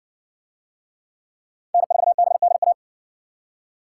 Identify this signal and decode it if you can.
Morse code, heard as a keyed tone, about 58 wpm, 700 Hz. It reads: N4BDA